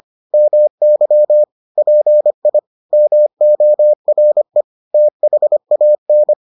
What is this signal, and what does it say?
Morse code, heard as a keyed tone, about 25 wpm, 600 Hz. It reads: MY PI MORE THAN